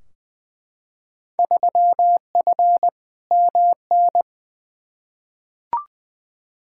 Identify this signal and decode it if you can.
Morse code, heard as a keyed tone, about 20 wpm, 700 Hz. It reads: 3F MN
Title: Field Day Exchanges Encore - 20wpm